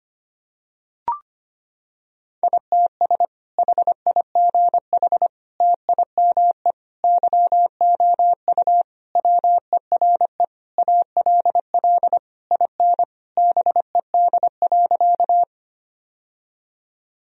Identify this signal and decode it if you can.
Morse code, heard as a keyed tone, about 25 wpm, 700 Hz. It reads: ITS HIGH TIME YOU WERE ALL IN BED.
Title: Sentences from Top 1000 Words - Review Part 1 - 25wpm